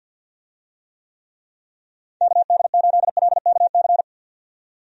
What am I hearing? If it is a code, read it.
Morse code, heard as a keyed tone, about 51 wpm, 700 Hz. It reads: KD8LKC
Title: USA Call Signs - 6 Characters - 50wpm